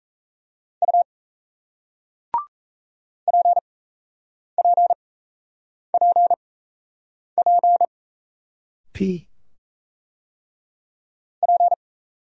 Morse code, heard as a keyed tone, about 37 wpm, spaced as for 8 wpm, 700 Hz. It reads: U PPPP P